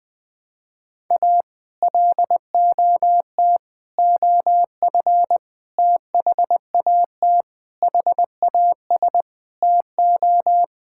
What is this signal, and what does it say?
Morse code, heard as a keyed tone, about 20 wpm, 700 Hz. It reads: A LOT OF THAT HAS TO
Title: Sets of 8 Words - 20wpm